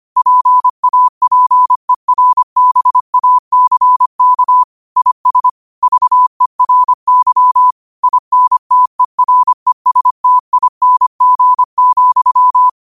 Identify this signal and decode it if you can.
Morse code, heard as a keyed tone, about 25 wpm, 1000 Hz. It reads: PAPERBACK IS VERY INTERESTING,
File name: qotdfiles / 25wpm